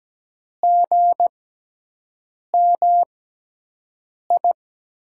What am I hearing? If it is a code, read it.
Morse code, heard as a keyed tone, about 17 wpm, spaced as for 6 wpm, 700 Hz. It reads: GMI
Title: USA Call Signs - 5 Characters - 6x Character Spacing - 17wpm